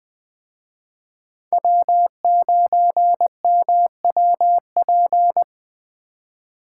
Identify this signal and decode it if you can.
Morse code, heard as a keyed tone, about 20 wpm, 700 Hz. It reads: W9MWP
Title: Speed Racing - USA 1x3 Call Signs 17wpm